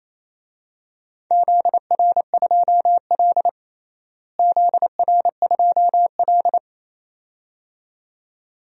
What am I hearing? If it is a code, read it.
Morse code, heard as a keyed tone, about 28 wpm, 700 Hz. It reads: ZR2L ZR2L
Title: DXCC-100 Call Signs - 4 Characters - Repeated 2x - 28wpm